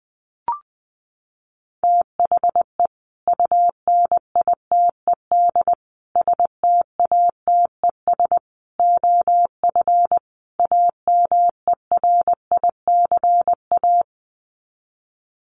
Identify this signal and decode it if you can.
Morse code, heard as a keyed tone, about 20 wpm, 700 Hz. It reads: THE UNITED STATES OF AMERICA
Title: Sets of 5 Words 20wpm